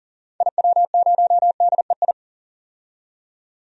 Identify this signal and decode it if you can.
Morse code, heard as a keyed tone, about 40 wpm, 700 Hz. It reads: IW0DEI
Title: DXCC-100 Call Signs - 6 Characters - 40wpm